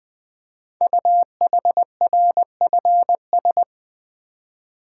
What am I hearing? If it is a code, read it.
Morse code, heard as a keyed tone, about 20 wpm, 700 Hz. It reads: UHRFS
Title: Code Groups - 5 Letters - 20wpm